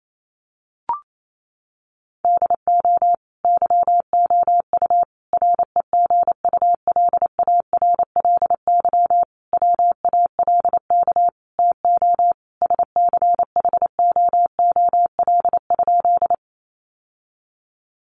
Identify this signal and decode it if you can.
Morse code, heard as a keyed tone, about 28 wpm, 700 Hz. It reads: DO YOU REGULARLY WALK TO SCHOOL?
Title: Sentences from Top 1400 Words 28wpm